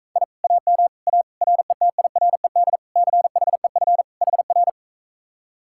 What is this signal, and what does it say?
Morse code, heard as a keyed tone, about 44 wpm, 700 Hz. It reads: I AM A RETIRED CHEF HR